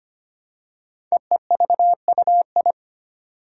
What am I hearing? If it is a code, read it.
Morse code, heard as a keyed tone, about 25 wpm, 700 Hz. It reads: EEVUI
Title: Code Groups - 5 Letters - 25wpm